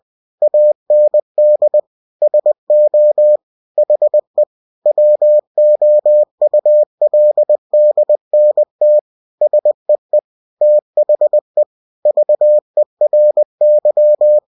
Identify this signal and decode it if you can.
Morse code, heard as a keyed tone, about 20 wpm, 600 Hz. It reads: AND SO HE WOULDNT SEE THE VERY
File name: winnie-20wpm